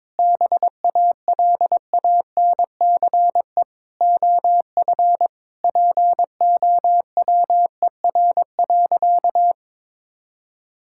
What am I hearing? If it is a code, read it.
Morse code, heard as a keyed tone, about 22 wpm, 700 Hz. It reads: BALANCE OF POWER.
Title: Sentences from Top 1000 Words - Review Part 6 - 22wpm